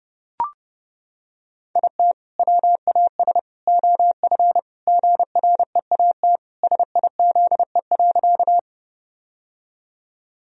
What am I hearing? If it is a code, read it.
Morse code, heard as a keyed tone, about 32 wpm, 700 Hz. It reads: IT WAS OF GREAT SIZE.
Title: Sentences from Top 1000 Words - Review Part 2 - 30wpm